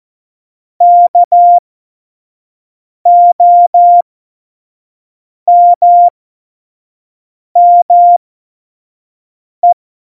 Morse code, heard as a keyed tone, about 14 wpm, spaced as for 5 wpm, 700 Hz. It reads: KOMME
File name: Morsekurs / Witze / Witz 04. September 2018, 25 F70